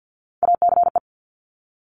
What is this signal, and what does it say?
Morse code, heard as a keyed tone, about 50 wpm, 690 Hz. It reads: ACE
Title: Top Three Letter Combinations in Words 50wpm